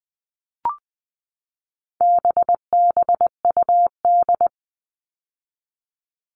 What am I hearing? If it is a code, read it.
Morse code, heard as a keyed tone, about 20 wpm, 700 Hz. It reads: BBUD
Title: All Cut Numbers - 4-Digit 20wpm